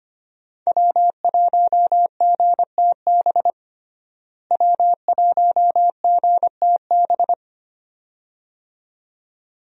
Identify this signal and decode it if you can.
Morse code, heard as a keyed tone, about 25 wpm, 700 Hz. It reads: W1GTB W1GTB
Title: USA Call Signs - 5 Characters - Repeated 2x - 25wpm